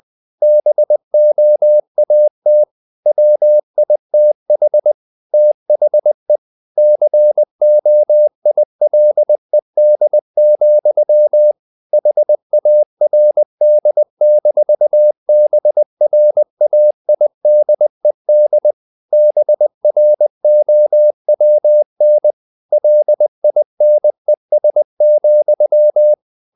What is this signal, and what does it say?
Morse code, heard as a keyed tone, about 20 wpm, 600 Hz. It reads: BOAT WITH THE COILED, HARD-BRAIDED BROWN LINES,
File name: the old man and the sea / the old man and the sea part 1 20wpm